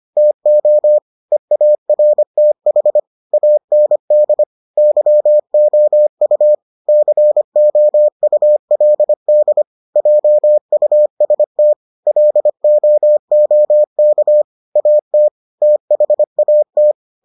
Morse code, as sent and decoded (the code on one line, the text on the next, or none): - --- / . .- .-. - .... / .- -. -.. / -.-- --- ..- / -.-. --- ..- .-.. -.. / .--- ..- ... - / .-.. --- --- -.- / .- - / - .... .- -
TO EARTH AND YOU COULD JUST LOOK AT THAT